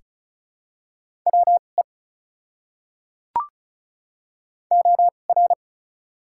Morse code, heard as a keyed tone, about 36 wpm, spaced as for 25 wpm, 700 Hz. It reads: WE OR
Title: 2x Character Spacing - Top 300 Words - 35wpm